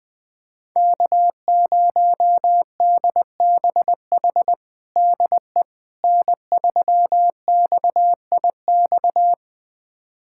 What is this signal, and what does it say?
Morse code, heard as a keyed tone, about 21 wpm, 700 Hz. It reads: K0DBH DE N3XIX
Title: USA Callsign DE Callsign - 5 Characters - 20wpm